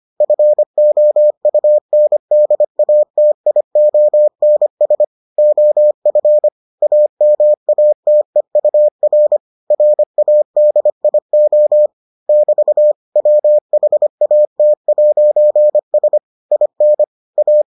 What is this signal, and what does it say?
Morse code, heard as a keyed tone, about 25 wpm, 600 Hz. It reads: FOUNDATIONS OF AMATEUR RADIO <BT> WHAT'S IN A